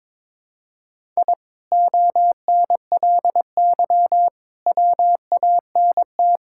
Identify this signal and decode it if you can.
Morse code, heard as a keyed tone, about 22 wpm, 700 Hz. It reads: I ONLY WANT
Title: Sentences from Top 1000 Words - Review Part 6 - 22wpm